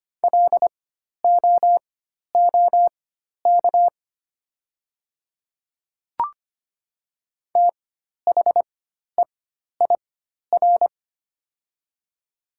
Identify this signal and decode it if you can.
Morse code, heard as a keyed tone, about 25 wpm, spaced as for 12 wpm, 700 Hz. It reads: LOOK THEIR
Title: 4x Character Spacing - Top 300 Words - 25wpm